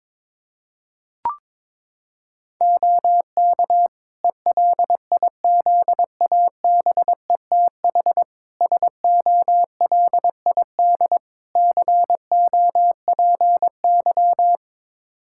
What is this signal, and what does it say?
Morse code, heard as a keyed tone, about 22 wpm, 700 Hz. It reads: OK ELIZABETH SOLID COPY
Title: Practice QSO - Exchange 3 - More Info - 22wpm